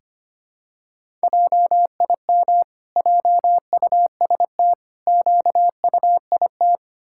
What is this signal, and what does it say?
Morse code, heard as a keyed tone, about 25 wpm, 700 Hz. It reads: JIM JUST QUIT
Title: Pangram Challenge v2 - 25wpm